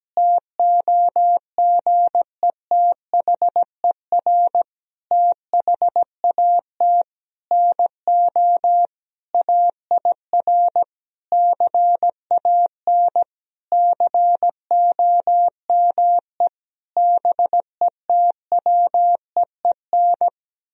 Morse code, heard as a keyed tone, about 17 wpm, 700 Hz. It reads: TOGETHER THAT NO AIR CAN COME BETWEEN
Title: Sentences from Top 1000 Words - Review Part 1 - 17wpm